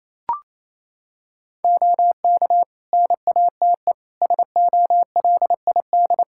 Morse code, heard as a keyed tone, about 28 wpm, 700 Hz. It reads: OK NATE SOLID
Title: Practice QSO - Exchange 3 - More Info - 28wpm